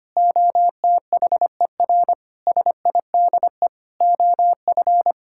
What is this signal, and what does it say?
Morse code, heard as a keyed tone, about 25 wpm, 700 Hz. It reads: OTHER SIDE OF
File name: practice / 25wpm / Sentences with 3-Word Phrases - 25wpm